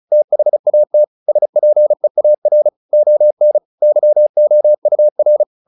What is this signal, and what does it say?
Morse code, heard as a keyed tone, about 36 wpm, 600 Hz. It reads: THAT SPEAR ON YOUR
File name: the old man and the sea 2 35wpm